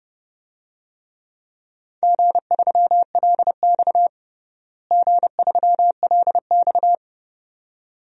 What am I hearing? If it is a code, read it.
Morse code, heard as a keyed tone, about 30 wpm, 700 Hz. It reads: G3LX G3LX
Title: DXCC-100 Call Signs - 4 Characters - Repeated 2x - 30wpm